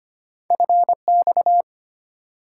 Morse code, heard as a keyed tone, about 25 wpm, 700 Hz. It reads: FX
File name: Two Letter-Number Combinations - 25wpm